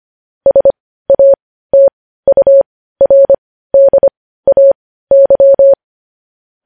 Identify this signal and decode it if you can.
Morse code, heard as a keyed tone, about 25 wpm, spaced as for 15 wpm, 550 Hz. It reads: SATURDAY
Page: news headlines in morse code at 15 wpm